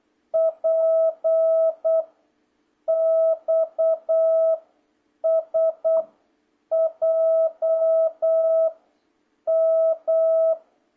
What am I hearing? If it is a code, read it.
Morse code, heard as a keyed tone, about 8 wpm, 660 Hz. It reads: P X S J M